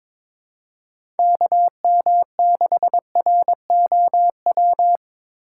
Morse code, heard as a keyed tone, about 22 wpm, 700 Hz. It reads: KM6ROW
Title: USA Call Signs - 6 Characters - 2x Character Spacing - 22wpm